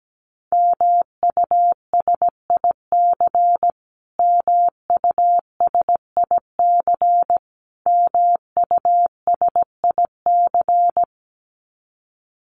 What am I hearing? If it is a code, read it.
Morse code, heard as a keyed tone, about 17 wpm, 700 Hz. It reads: MUSIC MUSIC MUSIC
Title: Top 500 Words Repeated 17wpm